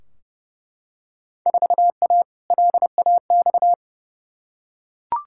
Morse code, heard as a keyed tone, about 32 wpm, 700 Hz. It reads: 4A LAX
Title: Field Day Exchanges - 30wpm